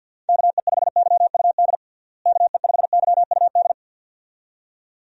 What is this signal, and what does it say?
Morse code, heard as a keyed tone, about 49 wpm, 700 Hz. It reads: KE5YUD KE5YUD
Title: USA Call Signs - 6 Characters - Repeated 2x - 50wpm